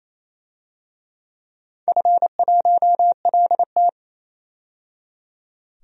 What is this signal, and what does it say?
Morse code, heard as a keyed tone, about 28 wpm, 700 Hz. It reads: F1LT
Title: DXCC-100 Call Signs - 4 Characters - 28wpm